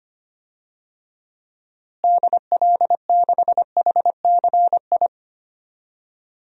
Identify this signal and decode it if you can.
Morse code, heard as a keyed tone, about 25 wpm, 700 Hz. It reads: DL6HCI
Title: DXCC-100 Call Signs - 6 Characters - Repeated 3x - 25wpm